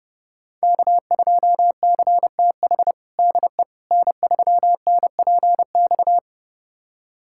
Morse code, heard as a keyed tone, about 30 wpm, 700 Hz. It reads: K2CTH DE N3NPX
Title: USA Callsign DE Callsign - 5 Characters - 30wpm